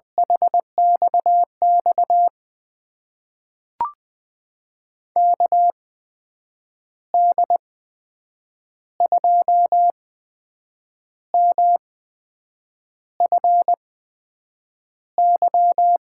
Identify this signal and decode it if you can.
Morse code, heard as a keyed tone, about 21 wpm, spaced as for 7 wpm, 700 Hz. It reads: HXX KD2MFY
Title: USA Call Signs - 6 Characters - 8x Character Spacing - 20wpm